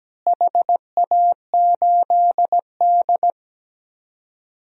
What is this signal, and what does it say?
Morse code, heard as a keyed tone, about 17 wpm, 700 Hz. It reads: HA8D
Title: DXCC-100 Call Signs - 4 Characters - Repeated 3x - 17wpm